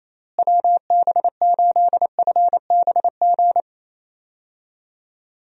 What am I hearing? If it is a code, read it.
Morse code, heard as a keyed tone, about 28 wpm, 700 Hz. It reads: WB8FBG
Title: USA Call Signs - 6 Characters - Repeated 2x - 28wpm